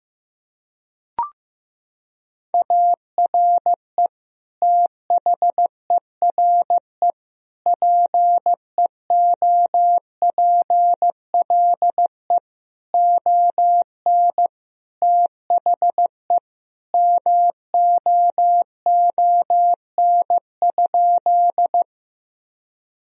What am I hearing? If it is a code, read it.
Morse code, heard as a keyed tone, about 15 wpm, 700 Hz. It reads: ARE THERE PEOPLE ON THE MOON?
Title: Sentences from Top 1000 Words - Review Part 3 - 15wpm